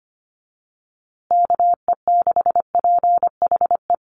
Morse code, heard as a keyed tone, about 25 wpm, 700 Hz. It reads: KE6PHE
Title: USA Call Signs Repeated 3x 25wpm